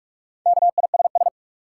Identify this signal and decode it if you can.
Morse code, heard as a keyed tone, about 46 wpm, 700 Hz. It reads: KISS
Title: Four Letter Words - 45wpm